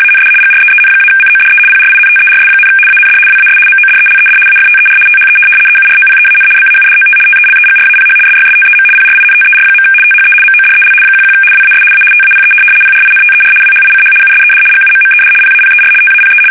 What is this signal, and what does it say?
RTTY, about 75 baud. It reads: AND GREEN-CODE PRINTERS, AND WAS MODELED AFTER THE EUROPEAN BAUDOT TELEGRAPH SYSTEM PRINTER. THE MODEL 11 WAS A TAPE PRINTER WHICH USED GUMMED PAPER TAPE
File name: fsk850-75B / teletype-corp